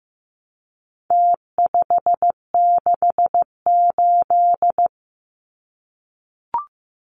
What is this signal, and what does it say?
Morse code, heard as a keyed tone, about 15 wpm, 700 Hz. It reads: T568
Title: Numbers - Contest Style - 4-Digit 15wpm